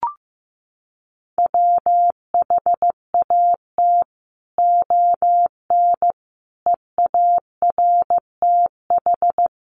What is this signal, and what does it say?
Morse code, heard as a keyed tone, about 15 wpm, 700 Hz. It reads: WHAT ON EARTH
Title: Sentences from Top 1100 Words 15wpm